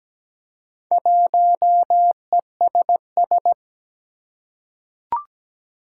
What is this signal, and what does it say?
Morse code, heard as a keyed tone, about 17 wpm, 700 Hz. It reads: 1ESS
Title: Four Challenging Letter-Number Combinations 17wpm